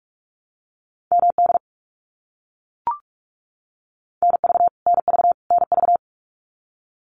Morse code, heard as a keyed tone, about 44 wpm, 700 Hz. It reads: MD NV NV NV
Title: US States Repeated 3x 45wpm